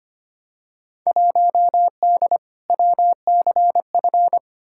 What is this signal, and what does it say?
Morse code, heard as a keyed tone, about 25 wpm, 700 Hz. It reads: 1D WCF
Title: Field Day Exchanges - 25wpm